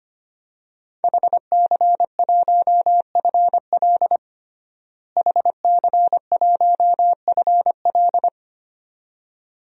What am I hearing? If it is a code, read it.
Morse code, heard as a keyed tone, about 25 wpm, 700 Hz. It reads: HC1FL HC1FL
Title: DXCC-100 Call Signs - 5 Characters - Repeated 2x - 25wpm